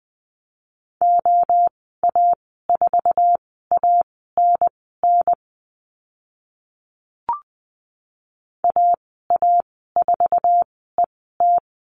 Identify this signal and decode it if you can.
Morse code, heard as a keyed tone, about 20 wpm, spaced as for 14 wpm, 700 Hz. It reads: OA4ANN AA4ET
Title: 055 Lesson 10g - Letter r - Call Signs 20wpm